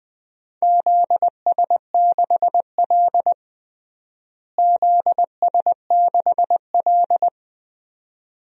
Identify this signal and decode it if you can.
Morse code, heard as a keyed tone, about 20 wpm, 700 Hz. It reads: ZS6L ZS6L
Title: DXCC-100 Call Signs - 4 Characters - Repeated 2x - 20wpm